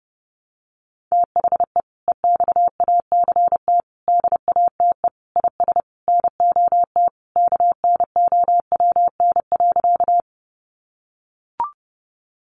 Morse code, heard as a keyed tone, about 30 wpm, 700 Hz. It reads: THE EXACT DATE IS NOT KNOWN.
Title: Sentences from Top 1300 Words 30wpm